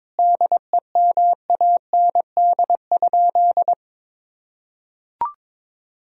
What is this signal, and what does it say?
Morse code, heard as a keyed tone, about 22 wpm, 700 Hz. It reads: DEMAND?
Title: Sentences from Top 1000 Words - Review Part 4 - 22wpm